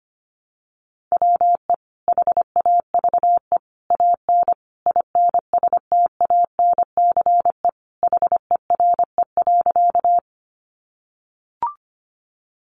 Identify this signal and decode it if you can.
Morse code, heard as a keyed tone, about 25 wpm, 700 Hz. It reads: WE HAVE AN INSTANCE HERE.
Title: Sentences from Top 800 Words 25wpm